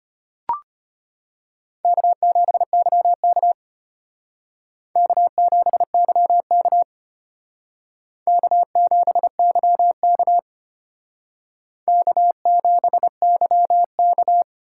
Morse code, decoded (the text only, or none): K7YK K7YK K7YK K7YK